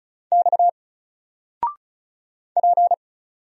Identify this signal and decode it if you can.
Morse code, heard as a keyed tone, about 35 wpm, 700 Hz. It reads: X P
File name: Single-Letter-Number-35wpm